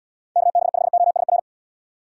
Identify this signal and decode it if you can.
Morse code, heard as a keyed tone, about 78 wpm, 700 Hz. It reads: KB5OSR